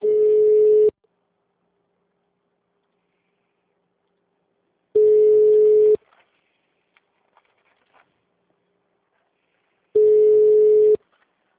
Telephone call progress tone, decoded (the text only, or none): ringback tone